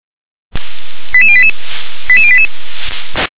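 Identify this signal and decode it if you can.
Five-tone selective call (ZVEI), heard as quick runs of five tones, tones of about 70 ms. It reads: 88088 88088
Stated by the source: ZVEI five-tone call 88503 received by SDR.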